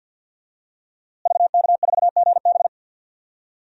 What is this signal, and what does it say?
Morse code, heard as a keyed tone, about 49 wpm, 700 Hz. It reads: VK4GB
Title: DXCC-100 Call Signs - 5 Characters - Repeated 2x - 50wpm